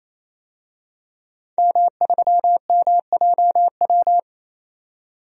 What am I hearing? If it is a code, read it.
Morse code, heard as a keyed tone, about 28 wpm, 700 Hz. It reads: M3MJW
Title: DXCC-100 Call Signs - 5 Characters - 28wpm